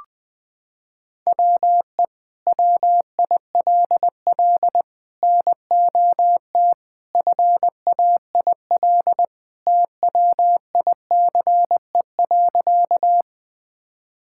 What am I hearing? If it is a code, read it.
Morse code, heard as a keyed tone, about 20 wpm, 700 Hz. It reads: WE WILL NOT FAIL TWICE.
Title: Sentences from Top 1600 Words 20wpm